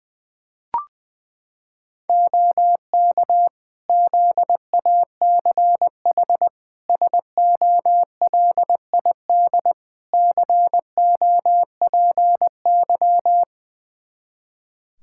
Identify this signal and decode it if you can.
Morse code, heard as a keyed tone, about 21 wpm, 700 Hz. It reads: OK ZACH SOLID COPY
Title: K1USN SST QSO Practice 20wpm